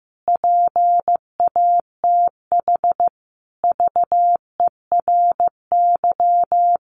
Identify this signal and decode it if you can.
Morse code, heard as a keyed tone, about 15 wpm, 700 Hz. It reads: PATH VERY
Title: Sentences from Top 1400 Words 15wpm